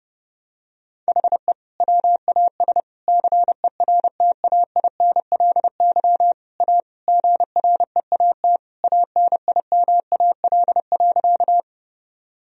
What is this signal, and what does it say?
Morse code, heard as a keyed tone, about 30 wpm, 700 Hz. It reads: HE WAS CERTAINLY A GREAT ANIMAL.
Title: Sentences from Top 1000 Words - Review Part 1 - 30wpm